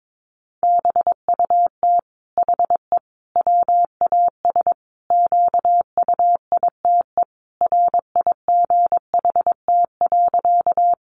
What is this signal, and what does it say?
Morse code, heard as a keyed tone, about 22 wpm, 700 Hz. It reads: BUT HE WAS QUITE RIGHT.
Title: Sentences from Top 700 Words 22wpm